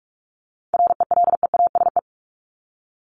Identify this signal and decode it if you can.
Morse code, heard as a keyed tone, about 44 wpm, 700 Hz. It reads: RELEASE
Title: Top 1001-1250 Words 45wpm